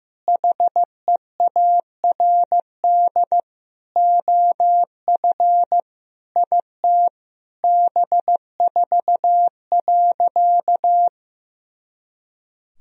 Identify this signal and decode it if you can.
Morse code, heard as a keyed tone, about 15 wpm, 700 Hz. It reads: HEARD OF IT B4.
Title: Sentences from Top 400 Words and Abbreviations 15wpm